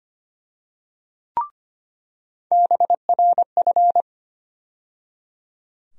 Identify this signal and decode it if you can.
Morse code, heard as a keyed tone, about 25 wpm, 700 Hz. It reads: BRF
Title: USA Call Sign Postfixes - 3 Letters 25wpm